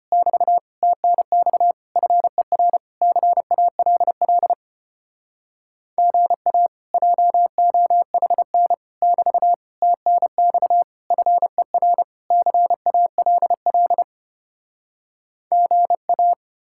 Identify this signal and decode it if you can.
Morse code, heard as a keyed tone, about 33 wpm, 700 Hz. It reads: <BT> TNX FER CALL GA JOHN <BT> TNX FER CALL GA